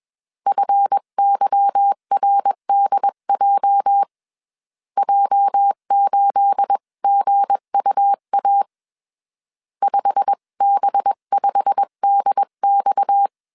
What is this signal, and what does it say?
Morse code, heard as a keyed tone, about 21 wpm, 800 Hz. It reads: FYRDJ J8GUA 5B5DX